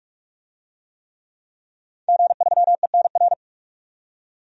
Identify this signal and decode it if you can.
Morse code, heard as a keyed tone, about 41 wpm, 700 Hz. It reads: G3ENR
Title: DXCC-100 Call Signs - 5 Characters - 45wpm